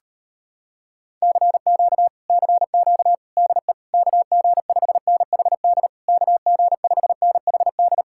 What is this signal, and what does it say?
Morse code, heard as a keyed tone, about 33 wpm, 700 Hz. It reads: CQ CQ DE KG5NHD KG5NHD